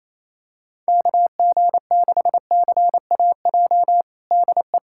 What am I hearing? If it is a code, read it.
Morse code, heard as a keyed tone, about 28 wpm, 700 Hz. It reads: KG6CAJ DE
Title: USA Callsign DE Callsign - 6 Characters - 28wpm